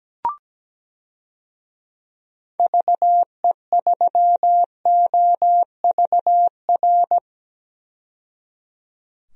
Morse code, heard as a keyed tone, about 17 wpm, 700 Hz. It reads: VE3OVR
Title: DXCC-100 Call Signs - 6 Characters - 17wpm